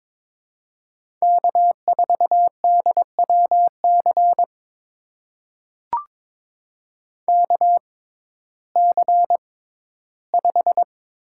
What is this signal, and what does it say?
Morse code, heard as a keyed tone, about 22 wpm, spaced as for 13 wpm, 700 Hz. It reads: K4DWC KC5
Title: USA Call Signs - 5 Characters - 6x Character Spacing - 22wpm